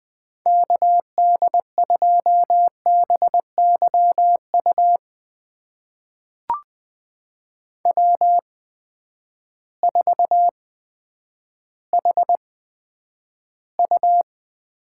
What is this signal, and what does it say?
Morse code, heard as a keyed tone, about 20 wpm, spaced as for 10 wpm, 700 Hz. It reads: KD2BYU W4HU
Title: USA Call Signs - 8x Character Spacing - 20wpm